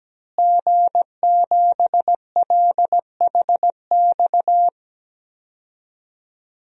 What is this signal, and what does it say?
Morse code, heard as a keyed tone, about 17 wpm, 700 Hz. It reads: G7LHX
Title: DXCC-100 Call Signs - 5 Characters - Repeated 3x - 17wpm